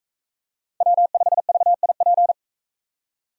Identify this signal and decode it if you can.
Morse code, heard as a keyed tone, about 44 wpm, 700 Hz. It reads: W5VIP